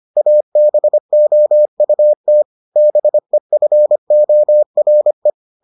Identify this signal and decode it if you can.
Morse code, heard as a keyed tone, about 25 wpm, 600 Hz. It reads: ABOUT BEFORE